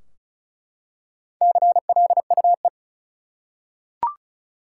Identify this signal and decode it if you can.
Morse code, heard as a keyed tone, about 34 wpm, 700 Hz. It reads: CLUE
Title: Four Letter Words - 35wpm